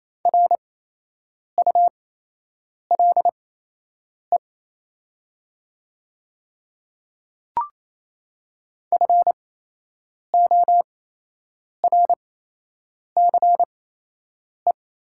Morse code, heard as a keyed tone, about 27 wpm, spaced as for 8 wpm, 690 Hz. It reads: RULE FORCE
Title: 8x Character Spacing - Top 500 Words - 28wpm